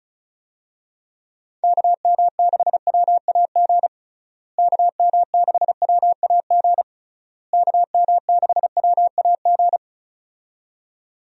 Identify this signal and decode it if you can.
Morse code, heard as a keyed tone, about 34 wpm, 700 Hz. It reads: KM6WAG KM6WAG KM6WAG
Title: USA Call Signs - 6 Characters - Repeated 3x - 35wpm